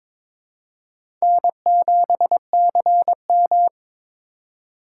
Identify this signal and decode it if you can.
Morse code, heard as a keyed tone, about 21 wpm, 700 Hz. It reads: N7CM
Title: USA Call Signs - 4 Characters - 4x Character Spacing - 22wpm